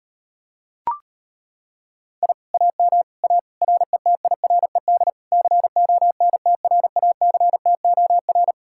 Morse code, heard as a keyed tone, about 33 wpm, 700 Hz. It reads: I AM A RETIRED CONTRACTOR